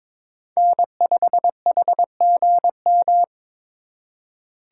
Morse code, heard as a keyed tone, about 21 wpm, 700 Hz. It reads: N5HGM
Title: USA Call Signs - 4x Character Spacing - 22wpm